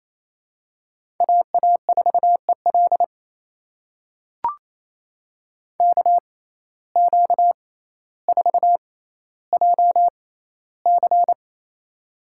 Morse code, heard as a keyed tone, about 27 wpm, spaced as for 14 wpm, 700 Hz. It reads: AA4EL KQ4JC
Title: USA Call Signs - 5 Characters - 6x Character Spacing - 28wpm